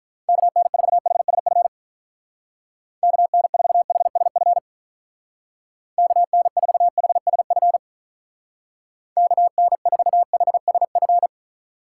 Morse code, heard as a keyed tone, about 44 wpm, 700 Hz. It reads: KN4HSF KN4HSF KN4HSF KN4HSF
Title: Speed Racing - USA 2x3 Call Signs 35wpm